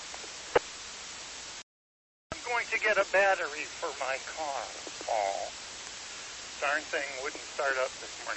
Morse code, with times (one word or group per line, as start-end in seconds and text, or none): none